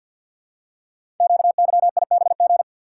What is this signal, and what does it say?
Morse code, heard as a keyed tone, about 50 wpm, 690 Hz. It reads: QYIBG
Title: Code Groups - 5 Letters - 50wpm